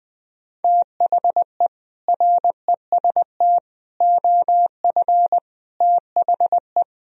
Morse code, heard as a keyed tone, about 20 wpm, 700 Hz. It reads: THE REST OF THE
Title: Speed-Racing - Sets of 5 Words 17wpm